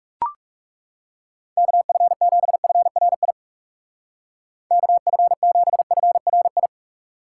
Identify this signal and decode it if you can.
Morse code, heard as a keyed tone, about 40 wpm, 700 Hz. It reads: KF7FRI KF7FRI